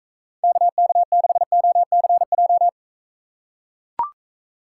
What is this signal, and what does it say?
Morse code, heard as a keyed tone, about 44 wpm, 700 Hz. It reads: KK6OCJ